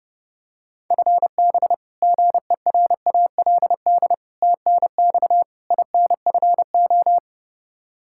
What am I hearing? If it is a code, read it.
Morse code, heard as a keyed tone, about 32 wpm, 700 Hz. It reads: FB GERALD TNX INFO
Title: Practice QSO - Exchange 3 - More Info - 30wpm